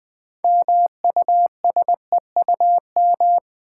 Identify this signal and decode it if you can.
Morse code, heard as a keyed tone, about 20 wpm, 700 Hz. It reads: MUSEUM
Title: 6 Letter Words - First 3 Letters Given - 20wpm